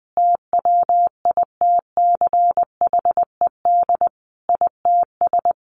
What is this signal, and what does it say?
Morse code, heard as a keyed tone, about 20 wpm, 700 Hz. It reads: TWITCHED ITS
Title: A Sound Of Thunder 20wpm